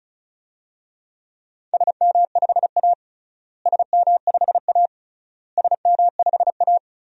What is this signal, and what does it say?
Morse code, heard as a keyed tone, about 36 wpm, 700 Hz. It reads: SM5A SM5A SM5A
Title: DXCC-100 Call Signs - 4 Characters - Repeated 3x - 35wpm